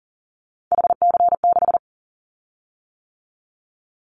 Morse code, heard as a keyed tone, about 40 wpm, 700 Hz. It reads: HC6
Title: Callsign Prefixes for Top 100 DXCC - 40wpm